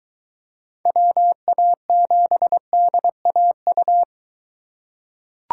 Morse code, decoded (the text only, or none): WA7DAU